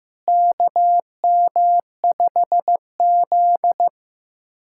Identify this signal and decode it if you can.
Morse code, heard as a keyed tone, about 15 wpm, 700 Hz. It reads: KM5Z
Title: USA Callsign DE Callsign - 4 Characters - 15wpm